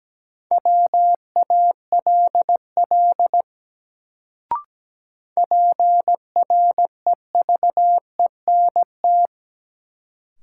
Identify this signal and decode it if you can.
Morse code, heard as a keyed tone, about 17 wpm, 700 Hz. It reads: WALL PREVENT
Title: Sentences from Top 1100 Words - New Word Drill Alt - 17wpm